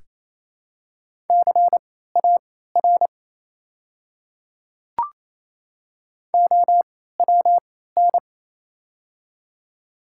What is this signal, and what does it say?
Morse code, heard as a keyed tone, about 28 wpm, spaced as for 16 wpm, 700 Hz. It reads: CAR OWN
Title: 3x Character Spacing - Top 500 Words - 28wpm